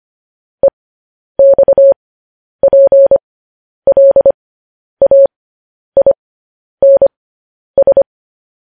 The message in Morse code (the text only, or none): EXPLAINS